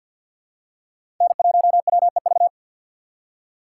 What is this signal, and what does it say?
Morse code, heard as a keyed tone, about 53 wpm, 690 Hz. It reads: N1WEV
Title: USA Call Signs - 2x Character Spacing - 50wpm